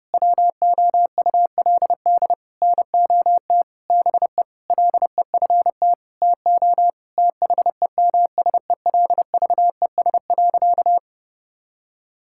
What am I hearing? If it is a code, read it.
Morse code, heard as a keyed tone, about 32 wpm, 700 Hz. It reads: WOULD NOT BE LEFT TO THEMSELVES.
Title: Sentences from Top 1000 Words - Review Part 5 - 30wpm